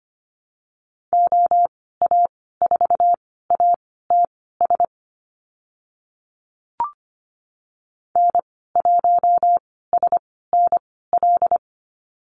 Morse code, heard as a keyed tone, about 25 wpm, spaced as for 16 wpm, 700 Hz. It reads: OA4ATS N1SNL
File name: Lesson 15g - Number 5 - Call Signs 25wpm